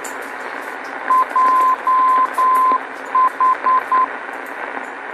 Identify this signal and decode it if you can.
Morse code, heard as a keyed tone, about 9 wpm, 1030 Hz, with radio noise. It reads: JH
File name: beacon program / tracks / 02 JH Beacon